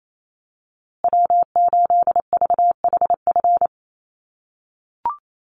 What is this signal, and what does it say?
Morse code, heard as a keyed tone, about 28 wpm, 700 Hz. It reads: W8VHF
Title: USA Call Signs Repeated 2x 28wpm